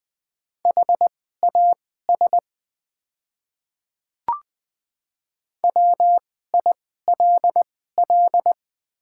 Morse code, heard as a keyed tone, about 20 wpm, spaced as for 14 wpm, 690 Hz. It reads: HAS WILL
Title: 2x Character Spacing - Top 100 Words - 20wpm